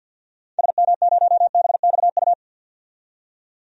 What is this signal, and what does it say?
Morse code, heard as a keyed tone, about 46 wpm, 700 Hz. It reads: SM0BXU
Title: DXCC-100 Call Signs - 6 Characters - 50wpm